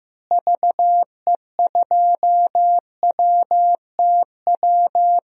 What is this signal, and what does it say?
Morse code, heard as a keyed tone, about 15 wpm, 700 Hz. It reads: VE2WTW
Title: DXCC-100 Call Signs - 6 Characters - Repeated 2x - 15wpm